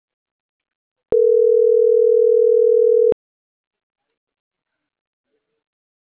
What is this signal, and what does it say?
Ringback tone (North America), 440 + 480 Hz.